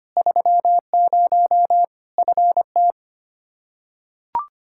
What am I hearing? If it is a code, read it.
Morse code, heard as a keyed tone, about 25 wpm, 700 Hz. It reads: 30 FT
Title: Practice QSO - Exchange 2 - Info - 25wpm